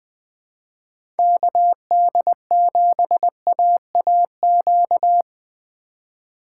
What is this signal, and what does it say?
Morse code, heard as a keyed tone, about 20 wpm, 700 Hz. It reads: KD7AAQ
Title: USA Call Signs - 4x Character Spacing - 20wpm